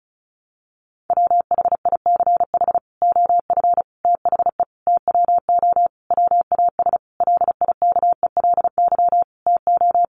Morse code, heard as a keyed tone, about 34 wpm, 700 Hz. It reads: WHICH OF THE TWO WAS LIKELY TO
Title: Sentences from Top 1300 Words 35wpm